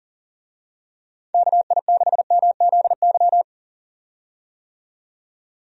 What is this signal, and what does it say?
Morse code, heard as a keyed tone, about 38 wpm, 700 Hz. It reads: KI6MZY
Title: USA Call Signs - 6 Characters - 40wpm